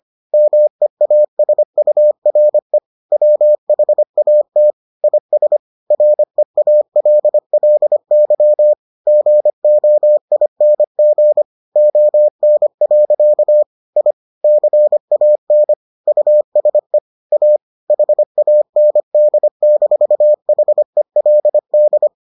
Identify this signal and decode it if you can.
Morse code, heard as a keyed tone, about 25 wpm, 600 Hz. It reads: MEASURE WHAT IS REALLY GOING ON. I CAN USE A HAND-HELD